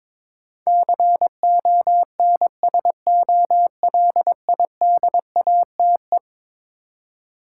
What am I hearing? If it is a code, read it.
Morse code, heard as a keyed tone, about 22 wpm, 700 Hz. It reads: CONSOLIDATE
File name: practice / 22wpm / Eleven Letter Words - 22wpm